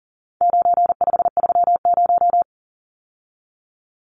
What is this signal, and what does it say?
Morse code, heard as a keyed tone, about 40 wpm, 700 Hz. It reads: 9530
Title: Numbers - 4-Digit 40wpm